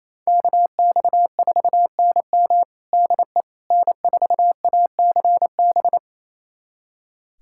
Morse code, heard as a keyed tone, about 28 wpm, 700 Hz. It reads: KX4NM DE N4ACB
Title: USA Callsign DE Callsign 28wpm